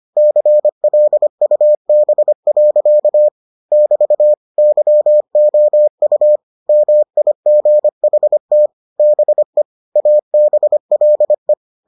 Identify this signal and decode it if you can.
Morse code, heard as a keyed tone, about 25 wpm, 600 Hz. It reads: CLUB. <BT> YOU MIGHT BE ABLE